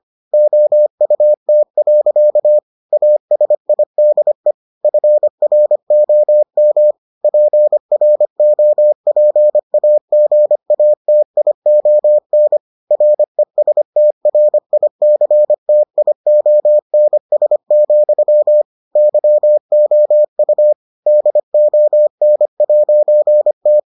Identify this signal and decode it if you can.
Morse code, heard as a keyed tone, about 25 wpm, 600 Hz. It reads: OUT. ASIDE FROM PROPAGATION RESTRICTIONS, YOU DON'T